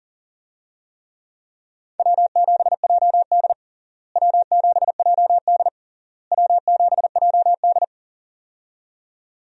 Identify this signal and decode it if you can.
Morse code, heard as a keyed tone, about 40 wpm, 700 Hz. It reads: W7JD W7JD W7JD
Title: USA Call Signs - 4 Characters - Repeated 3x - 40wpm